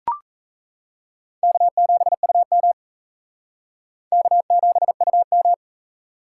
Morse code, heard as a keyed tone, about 36 wpm, 700 Hz. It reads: K7UM K7UM